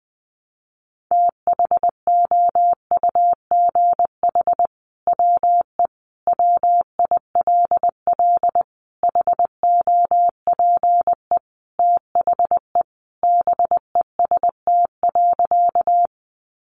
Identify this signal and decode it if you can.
Morse code, heard as a keyed tone, about 20 wpm, 700 Hz. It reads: THOUGH WE WILL HOPE THE BEST.
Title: Sentences from Top 1000 Words 20wpm